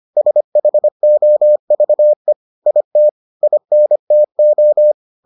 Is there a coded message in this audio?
Yes: Morse code from 0.2 to 4.9 s, heard as a keyed tone, about 25 wpm, 600 Hz. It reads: SHOVE IT INTO